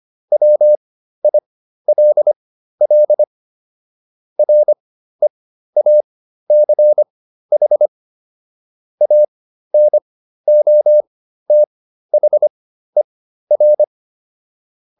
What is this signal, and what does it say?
Morse code, heard as a keyed tone, about 25 wpm, spaced as for 13 wpm, 600 Hz. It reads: WILL REACH ANOTHER